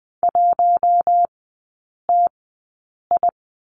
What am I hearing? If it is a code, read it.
Morse code, heard as a keyed tone, about 20 wpm, spaced as for 8 wpm, 700 Hz. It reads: 1TI
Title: 053 Lesson 10e - Letter r - Three Characters 20wpm